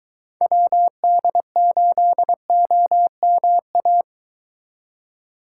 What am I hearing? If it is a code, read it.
Morse code, heard as a keyed tone, about 23 wpm, 700 Hz. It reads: WD8OMA